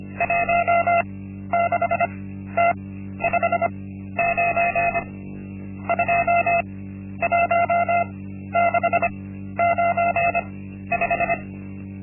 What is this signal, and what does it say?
Morse code, heard as a keyed tone, about 25 wpm, spaced as for 12 wpm, 690 Hz. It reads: 16T59 21695
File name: Linksmakalnis-1